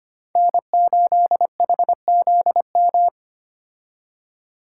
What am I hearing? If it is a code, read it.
Morse code, heard as a keyed tone, about 25 wpm, 700 Hz. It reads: N8HZM
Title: K1USN SST QSO Practice - 25wpm